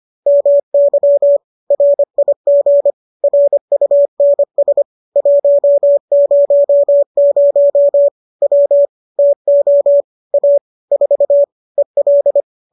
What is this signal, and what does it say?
Morse code, heard as a keyed tone, about 25 wpm, 570 Hz. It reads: MY RIG RUNS 100 W TO A 4 EL